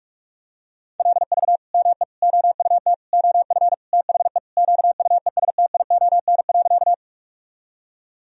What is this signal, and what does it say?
Morse code, heard as a keyed tone, about 42 wpm, 700 Hz. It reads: LV ME OUT OF THE QUESTION.
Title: Sentences from Top 400 Words and Abbreviations 45wpm